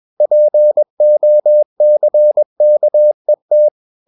Morse code, heard as a keyed tone, about 21 wpm, 600 Hz. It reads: POCKET
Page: CW med Gnister